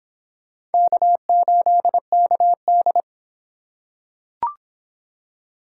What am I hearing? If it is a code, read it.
Morse code, heard as a keyed tone, about 25 wpm, 700 Hz. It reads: K8KD